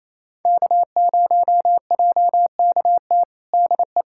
Morse code, heard as a keyed tone, about 28 wpm, 700 Hz. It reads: K0JKT DE
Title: USA Callsign DE Callsign 28wpm